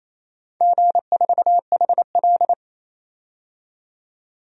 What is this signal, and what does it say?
Morse code, heard as a keyed tone, about 28 wpm, 700 Hz. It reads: G4HL
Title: DXCC-100 Call Signs - 4 Characters - 28wpm